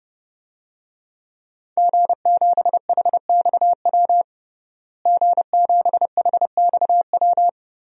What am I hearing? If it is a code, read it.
Morse code, heard as a keyed tone, about 30 wpm, 700 Hz. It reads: G7HXW G7HXW
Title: DXCC-100 Call Signs - 5 Characters - Repeated 2x - 30wpm